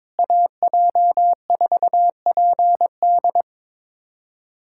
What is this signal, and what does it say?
Morse code, heard as a keyed tone, about 22 wpm, 700 Hz. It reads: AJ4PD
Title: USA Callsign DE Callsign - 5 Characters - 22wpm